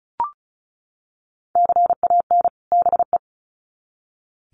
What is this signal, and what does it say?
Morse code, heard as a keyed tone, about 34 wpm, 700 Hz. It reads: CAN BE
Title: Sets of 2 Words 35wpm